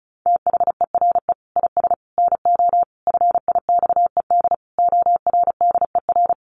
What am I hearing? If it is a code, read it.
Morse code, heard as a keyed tone, about 34 wpm, 700 Hz. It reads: THERE IS NO FIXED ORDER
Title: Sentences from Top 1100 Words 35wpm